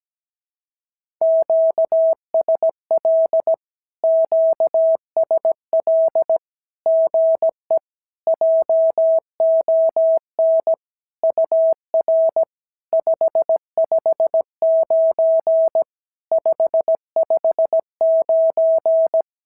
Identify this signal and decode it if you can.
Morse code, heard as a keyed tone, about 17 wpm, 650 Hz. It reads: QSL QSL GE JON UR 559 559
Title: SOTA QSO Practice 17wpm